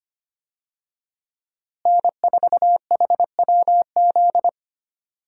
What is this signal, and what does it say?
Morse code, heard as a keyed tone, about 25 wpm, 700 Hz. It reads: N4HWZ
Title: USA Call Signs - 5 Characters - Repeated 2x - 25wpm